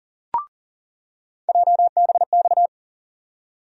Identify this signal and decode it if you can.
Morse code, heard as a keyed tone, about 40 wpm, 700 Hz. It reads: JBX
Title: USA Call Sign Postfixes - 3 Letters 40wpm